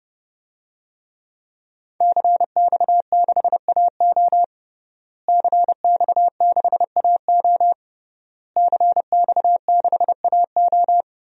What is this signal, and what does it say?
Morse code, heard as a keyed tone, about 32 wpm, 700 Hz. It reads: CX6AO CX6AO CX6AO
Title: DXCC-100 Call Signs - 5 Characters - Repeated 3x - 30wpm